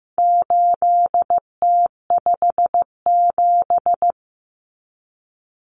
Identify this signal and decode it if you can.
Morse code, heard as a keyed tone, about 15 wpm, 700 Hz. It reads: 8T57
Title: Common Cut Numbers - 4-Digit 15wpm